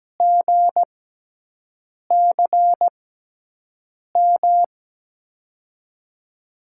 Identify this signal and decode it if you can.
Morse code, heard as a keyed tone, about 17 wpm, spaced as for 6 wpm, 700 Hz. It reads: GCM
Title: USA Call Signs - 5 Characters - 6x Character Spacing - 17wpm